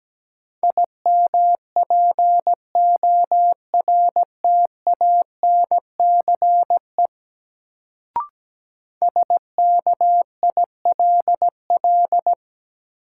Morse code, heard as a keyed tone, about 17 wpm, 700 Hz. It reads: IMPORTANCE SKILL
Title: Sentences from Top 1200 Words - New Word Drill - 17wpm